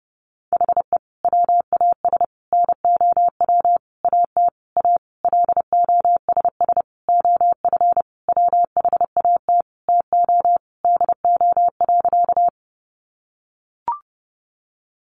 Morse code, heard as a keyed tone, about 30 wpm, 700 Hz. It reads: HE WAS NOW AT A LOSS OF WHAT TO DO.
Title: Sentences from Top 1400 Words 30wpm